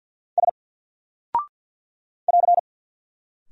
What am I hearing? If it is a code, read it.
Morse code, heard as a keyed tone, about 48 wpm, 700 Hz. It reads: S <AR>
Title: Single Character (Letter:Number:Punctuation:Prosign) 50wpm